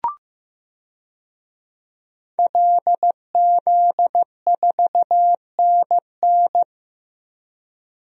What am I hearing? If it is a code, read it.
Morse code, heard as a keyed tone, about 15 wpm, 700 Hz. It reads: LZ4NN
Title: DXCC-100 Call Signs - 5 Characters - 15wpm